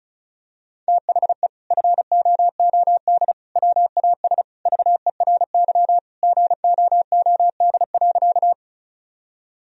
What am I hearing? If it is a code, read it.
Morse code, heard as a keyed tone, about 34 wpm, 700 Hz. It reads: THE FOOD WAS VERY GOOD.
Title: Sentences from Top 1000 Words - Review Part 1 - 35wpm